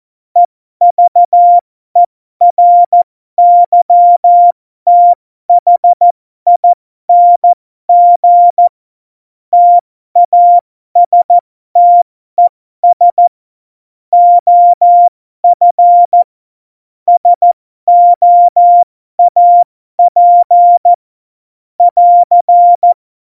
Morse code, heard as a keyed tone, about 14 wpm, spaced as for 12 wpm, 700 Hz. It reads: EVERYTHING TASTES OF SOAP <AR>
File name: Morsekurs / Witze